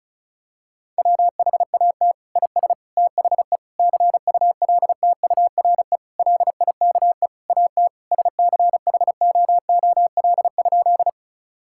Morse code, heard as a keyed tone, about 36 wpm, 700 Hz. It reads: WHAT IS THE CULTURE LIKE AT SCHOOL?
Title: Sentences from Top 1000 Words - Review Part 6 - 35wpm